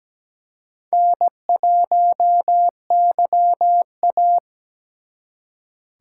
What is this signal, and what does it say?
Morse code, heard as a keyed tone, about 20 wpm, 700 Hz. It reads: N1YA
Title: Speed Racing - USA 1x2 Call Signs 17wpm